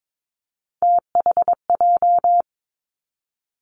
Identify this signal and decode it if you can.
Morse code, heard as a keyed tone, about 22 wpm, 700 Hz. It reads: THJ
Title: Three Challenging Letter-Number Combinations 22wpm